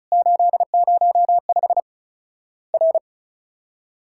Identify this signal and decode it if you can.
Morse code, heard as a keyed tone, about 36 wpm, 700 Hz. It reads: 805 R
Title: Worked All Europe DX Contest - QTC Exchange - 35wpm